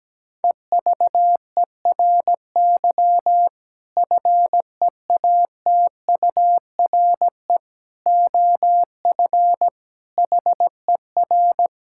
Morse code, heard as a keyed tone, about 17 wpm, 700 Hz. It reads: EVERY FEATURE OF HER
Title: Sentences from Top 1000 Words - Review Part 5 - 17wpm